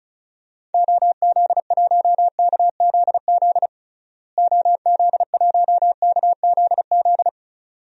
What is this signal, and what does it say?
Morse code, heard as a keyed tone, about 36 wpm, 700 Hz. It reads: OZ1KZZ OZ1KZZ
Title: DXCC-100 Call Signs - 6 Characters - Repeated 2x - 35wpm